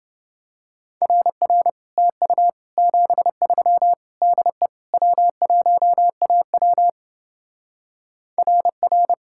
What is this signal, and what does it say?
Morse code, heard as a keyed tone, about 30 wpm, 700 Hz. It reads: RR TU 73 DE W1AW RR